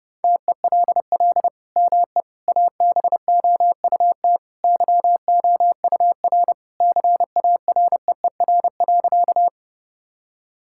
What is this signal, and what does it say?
Morse code, heard as a keyed tone, about 32 wpm, 700 Hz. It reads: TELL ME ABOUT YOUR CAREER.
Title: Sentences from Top 1000 Words - Review Part 6 - 30wpm